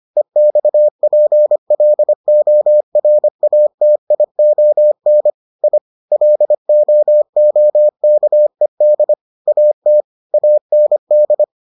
Morse code, heard as a keyed tone, about 25 wpm, 600 Hz. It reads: EXPLORATION I LOOKED AT AND